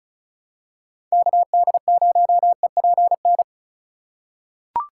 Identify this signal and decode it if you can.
Morse code, heard as a keyed tone, about 36 wpm, 700 Hz. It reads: KD0EPN
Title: USA Call Signs - 6 Characters - 2x Character Spacing - 35wpm